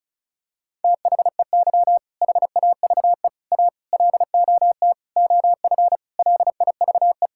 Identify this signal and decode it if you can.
Morse code, heard as a keyed tone, about 36 wpm, 700 Hz. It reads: THEY HAVE A LOT OF LIVE
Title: Sentences from Top 1000 Words - Review Part 5 - 35wpm